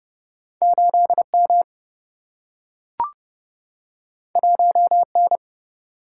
Morse code, heard as a keyed tone, about 30 wpm, 700 Hz. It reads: 8M 1N
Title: Single-Letter Single-Number 30wpm